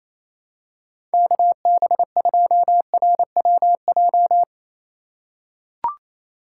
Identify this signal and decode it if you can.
Morse code, heard as a keyed tone, about 28 wpm, 700 Hz. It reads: KB2RWJ
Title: USA Call Signs - 6x Character Spacing - 28wpm